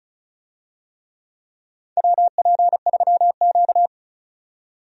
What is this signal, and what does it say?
Morse code, heard as a keyed tone, about 34 wpm, 700 Hz. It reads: WP3Q
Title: USA Call Signs - 4 Characters - 35wpm